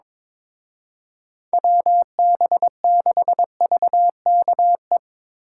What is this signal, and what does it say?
Morse code, heard as a keyed tone, about 24 wpm, 700 Hz. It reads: WB6VKE
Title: Speed Racing - USA 2x3 Call Signs 22wpm